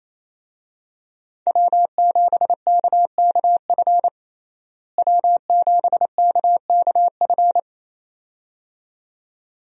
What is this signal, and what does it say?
Morse code, heard as a keyed tone, about 27 wpm, 700 Hz. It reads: W7KKF W7KKF
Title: USA Call Signs - 5 Characters - Repeated 2x - 28wpm